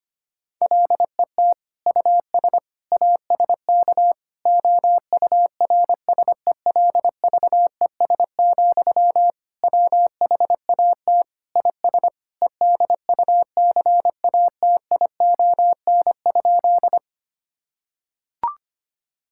Morse code, heard as a keyed tone, about 25 wpm, 700 Hz. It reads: LET US ASK OURSELVES, WHAT IS EDUCATION?
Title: Sentences from Top 1600 Words 25wpm